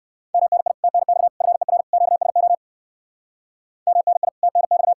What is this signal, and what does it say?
Morse code, heard as a keyed tone, about 55 wpm, 700 Hz. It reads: MNI TNX FER QSO MNI TNX
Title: Speed-Racing Common QSO Phrases - 45wpm